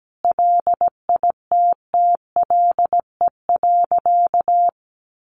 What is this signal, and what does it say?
Morse code, heard as a keyed tone, about 17 wpm, 700 Hz. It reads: LITTLE.
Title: Sentences from Top 800 Words 17wpm